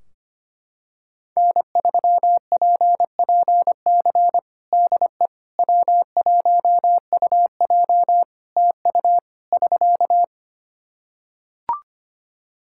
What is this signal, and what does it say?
Morse code, heard as a keyed tone, about 25 wpm, 700 Hz. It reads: N3PPC DE W1UJ TU <SK>
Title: Practice QSO - Exchange 4 - Ending - 25wpm